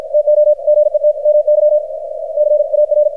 Morse code, heard as a keyed tone, about 53 wpm, 600 Hz, with radio noise. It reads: TO GET MY CAR
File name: archived / CRSnet